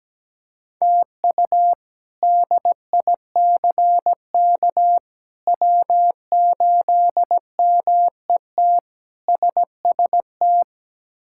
Morse code, heard as a keyed tone, about 17 wpm, 700 Hz. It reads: TU DICK W8MET SST
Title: K1USN SST QSO Practice - 17wpm